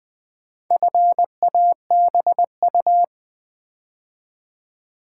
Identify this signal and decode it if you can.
Morse code, heard as a keyed tone, about 20 wpm, 700 Hz. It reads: FABU
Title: Build A Word - 4 Letters At A Time - 8-Letter Words - 20wpm